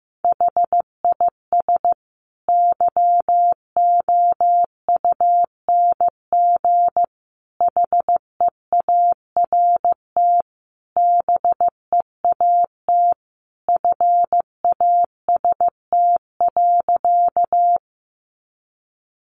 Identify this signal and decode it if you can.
Morse code, heard as a keyed tone, about 15 wpm, 700 Hz. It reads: HIS YOUNG HEART BEAT FAST.
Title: Sentences from Top 1300 Words 15wpm